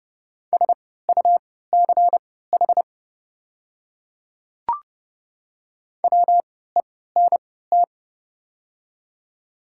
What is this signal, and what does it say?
Morse code, heard as a keyed tone, about 30 wpm, spaced as for 17 wpm, 690 Hz. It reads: SUCH WENT
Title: 3x Character Spacing - Top 200 Words - 30wpm